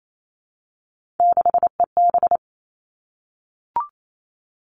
Morse code, heard as a keyed tone, about 28 wpm, 700 Hz. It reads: 6EB
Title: Three Challenging Letter-Number Combinations 28wpm